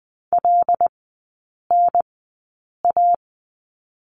Morse code, heard as a keyed tone, about 20 wpm, spaced as for 8 wpm, 700 Hz. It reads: LNA
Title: 127 Lesson 21e - Number 6 - Three Characters 20wpm